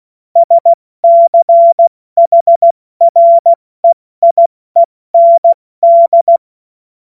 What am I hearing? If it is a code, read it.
Morse code, heard as a keyed tone, about 16 wpm, 680 Hz. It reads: SCHREIEND
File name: Morsekurs / Witze / Witz 24. Januar 2023 70 F80